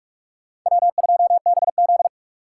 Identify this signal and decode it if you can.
Morse code, heard as a keyed tone, about 46 wpm, 700 Hz. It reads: W2BZ
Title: USA Call Signs - 4x Character Spacing - 45wpm